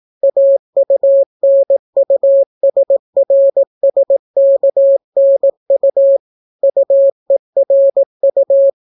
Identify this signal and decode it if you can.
Morse code, heard as a keyed tone, about 18 wpm, 550 Hz. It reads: AUNUSRSKNU UERU